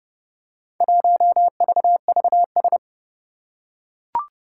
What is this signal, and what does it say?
Morse code, heard as a keyed tone, about 30 wpm, 690 Hz. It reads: 1VVS
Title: Four Challenging Letter-Number Combinations 30wpm